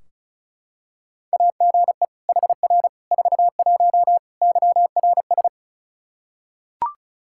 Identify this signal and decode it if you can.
Morse code, heard as a keyed tone, about 36 wpm, 700 Hz. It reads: AGE HR 41 YRS
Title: Practice QSO - Exchange 3 - More Info - 35wpm